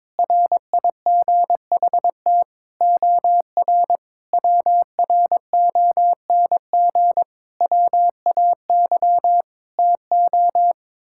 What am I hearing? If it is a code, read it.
Morse code, heard as a keyed tone, about 22 wpm, 700 Hz. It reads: RIGHT OR WRONG WAY TO
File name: practice / 22wpm / Sets of 8 Words - 22wpm